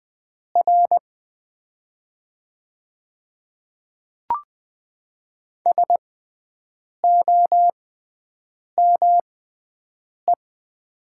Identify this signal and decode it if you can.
Morse code, heard as a keyed tone, about 20 wpm, spaced as for 7 wpm, 700 Hz. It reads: R SOME
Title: 6x Character Spacing - Top 200 Words - 20wpm